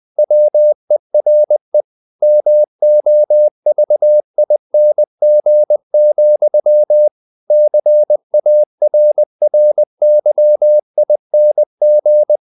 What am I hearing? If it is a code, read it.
Morse code, heard as a keyed tone, about 20 wpm, 600 Hz. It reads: WERE MOVING, CARRYING
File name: the old man and the sea / the old man and the sea part 1 20wpm